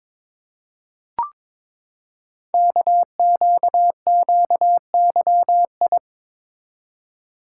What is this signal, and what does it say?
Morse code, heard as a keyed tone, about 22 wpm, 700 Hz. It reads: KQQYI
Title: Code Groups - 5 Letter:Numbers - 22wpm